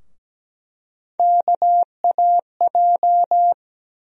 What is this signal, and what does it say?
Morse code, heard as a keyed tone, about 17 wpm, 700 Hz. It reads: KAJ
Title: USA Call Sign Postfixes - 3 Letters 17wpm